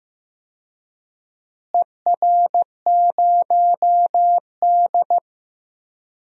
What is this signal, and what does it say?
Morse code, heard as a keyed tone, about 15 wpm, 700 Hz. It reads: ER0D
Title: DXCC-100 Call Signs - 4 Characters - Repeated 3x - 15wpm